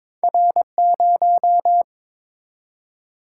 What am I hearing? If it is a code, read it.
Morse code, heard as a keyed tone, about 22 wpm, 700 Hz. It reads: R0
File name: practice / 22wpm / Two Letter-Number Combinations - 22wpm